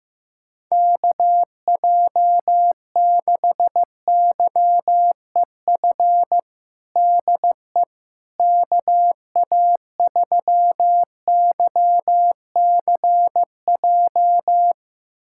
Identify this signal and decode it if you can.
Morse code, heard as a keyed tone, about 16 wpm, 700 Hz. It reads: KJ6YEF DE KA3YCJ
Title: USA Callsign DE Callsign 15wpm